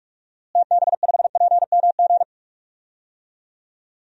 Morse code, heard as a keyed tone, about 42 wpm, 700 Hz. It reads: TB5PMG
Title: DXCC-100 Call Signs - 6 Characters - Repeated 3x - 45wpm